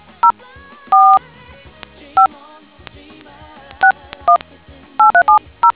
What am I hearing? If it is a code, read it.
DTMF, heard as telephone touch tones, tones of about 100 ms. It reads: 045 618370